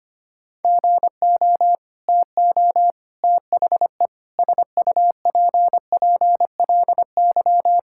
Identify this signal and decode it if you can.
Morse code, heard as a keyed tone, about 25 wpm, 700 Hz. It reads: GO TO THE SUPPLY
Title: Sentences from Top 1000 Words - Review Part 5 - 25wpm